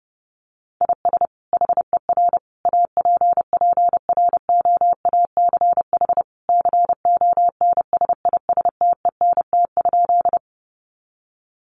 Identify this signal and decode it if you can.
Morse code, heard as a keyed tone, about 30 wpm, 700 Hz. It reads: IS HER APPROACH CONSISTENT?
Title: Sentences from Top 1500 Words 30wpm